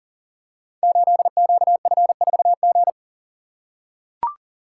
Morse code, heard as a keyed tone, about 42 wpm, 690 Hz. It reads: 8QF4G
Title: Code Groups - 5 Letter:Numbers - 40wpm